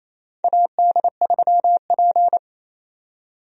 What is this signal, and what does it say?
Morse code, heard as a keyed tone, about 27 wpm, 700 Hz. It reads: AD3P
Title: USA Call Signs - 4 Characters - 28wpm